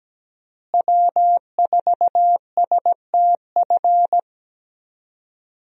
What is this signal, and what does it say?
Morse code, heard as a keyed tone, about 17 wpm, 700 Hz. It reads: W4STF
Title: USA Call Signs - 5 Characters - Repeated 2x - 17wpm